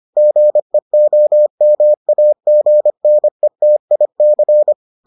Morse code, heard as a keyed tone, about 25 wpm, 600 Hz. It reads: GEOMAGNETIC